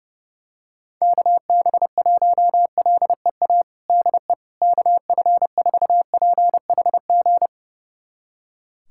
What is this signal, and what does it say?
Morse code, heard as a keyed tone, about 32 wpm, 700 Hz. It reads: KB1LEA DE KF4PHG
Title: USA Callsign DE Callsign 30wpm